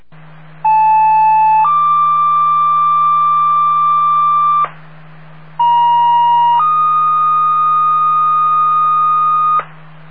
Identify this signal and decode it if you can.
Two-tone page, heard as a one-second tone, then a three-second tone, about 870 then 1180 Hz, about 950 then 1230 Hz. It is heard from 0.6 to 9.6 s.